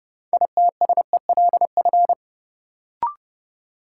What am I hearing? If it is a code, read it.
Morse code, heard as a keyed tone, about 30 wpm, 700 Hz. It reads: ITSELF
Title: Sentences from Top 800 Words - New Word Drill - 30wpm